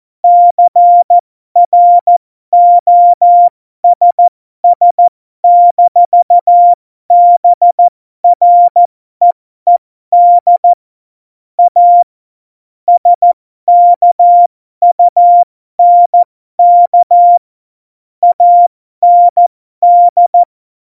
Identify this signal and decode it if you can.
Morse code, heard as a keyed tone, about 14 wpm, spaced as for 12 wpm, 700 Hz. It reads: CROSS-BREED A SKUNK AND
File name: Morsekurs / Witze